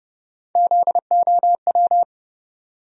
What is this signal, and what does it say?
Morse code, heard as a keyed tone, about 32 wpm, 700 Hz. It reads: ZOW
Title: USA Call Sign Postfixes - 3 Letters 30wpm